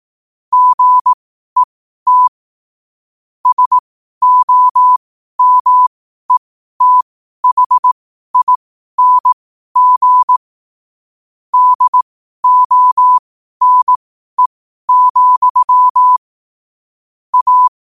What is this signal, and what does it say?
Morse code, heard as a keyed tone, about 18 wpm, spaced as for 12 wpm, 1000 Hz. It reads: GET SOMETHING DONE, A
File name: qotdfiles / 12wpm